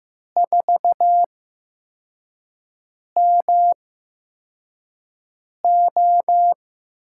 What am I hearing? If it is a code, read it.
Morse code, heard as a keyed tone, about 15 wpm, spaced as for 4 wpm, 700 Hz. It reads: 4MO
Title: USA Call Signs - 8x Character Spacing - 15wpm